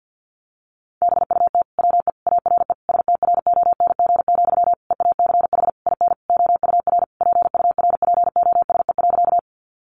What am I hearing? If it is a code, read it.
Morse code, heard as a keyed tone, about 50 wpm, 700 Hz. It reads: BUT WE ARE STRONG, EACH IN OUR PURPOSE.
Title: Sentences from Top 800 Words 50wpm